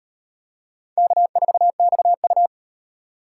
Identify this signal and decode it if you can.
Morse code, heard as a keyed tone, about 38 wpm, 700 Hz. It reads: K4XU